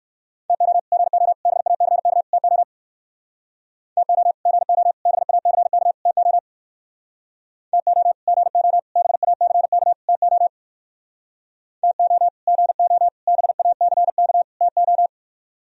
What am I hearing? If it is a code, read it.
Morse code, heard as a keyed tone, about 58 wpm, 700 Hz. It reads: TO GO BACK TO TO GO BACK TO TO GO BACK TO TO GO BACK TO